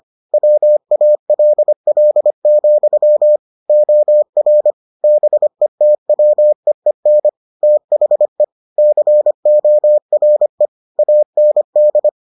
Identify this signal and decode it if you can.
Morse code, heard as a keyed tone, about 25 wpm, 600 Hz. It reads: WALL, OR BETWEEN THE CORE AND